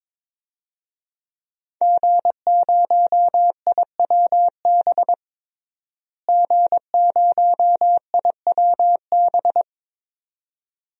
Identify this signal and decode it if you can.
Morse code, heard as a keyed tone, about 22 wpm, 700 Hz. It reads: G0IWB G0IWB
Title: DXCC-100 Call Signs - 5 Characters - Repeated 2x - 22wpm